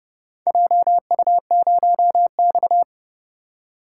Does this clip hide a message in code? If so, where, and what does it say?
Yes: Morse code from 0.5 to 2.8 s, heard as a keyed tone, about 29 wpm, 700 Hz. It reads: JU0X